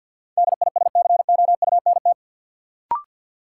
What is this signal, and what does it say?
Morse code, heard as a keyed tone, about 51 wpm, 700 Hz. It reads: DISCOUNT
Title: Sentences from Top 1200 Words - New Word Drill Alt - 50wpm